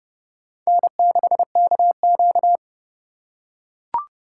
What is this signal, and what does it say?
Morse code, heard as a keyed tone, about 31 wpm, 700 Hz. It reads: N6KQ
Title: USA Call Signs - 4 Characters - 3x Character Spacing - 30wpm